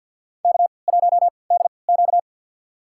Morse code, heard as a keyed tone, about 47 wpm, spaced as for 28 wpm, 700 Hz. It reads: K1DQ
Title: USA Call Signs - 4 Characters - 3x Character Spacing - 50wpm